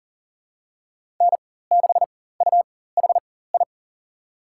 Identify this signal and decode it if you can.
Morse code, heard as a keyed tone, about 40 wpm, spaced as for 19 wpm, 700 Hz. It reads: N6UHI
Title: USA Call Signs - 5 Characters - 4x Character Spacing - 40wpm